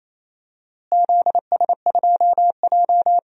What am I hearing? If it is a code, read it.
Morse code, heard as a keyed tone, about 28 wpm, 700 Hz. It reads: ZS2J
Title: DXCC-100 Call Signs - 4 Characters - Repeated 3x - 28wpm